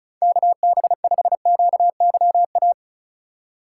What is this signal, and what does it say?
Morse code, heard as a keyed tone, about 36 wpm, 690 Hz. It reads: KB5QYA
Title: USA Call Signs - 6 Characters - 4x Character Spacing - 35wpm